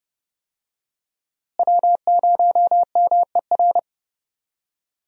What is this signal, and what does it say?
Morse code, heard as a keyed tone, about 30 wpm, 700 Hz. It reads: W0MER
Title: USA Call Signs - 5 Characters - Repeated 3x - 30wpm